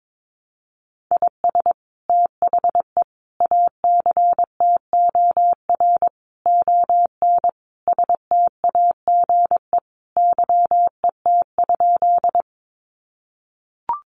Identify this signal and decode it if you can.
Morse code, heard as a keyed tone, about 22 wpm, 700 Hz. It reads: IS THE ACTOR ON STAGE YET?
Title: Sentences from Top 1500 Words 22wpm